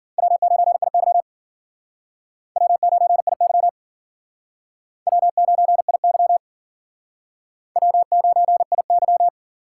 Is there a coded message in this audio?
Yes: Morse code from 0.2 to 9.3 s, heard as a keyed tone, about 51 wpm, 700 Hz. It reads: W9IY W9IY W9IY W9IY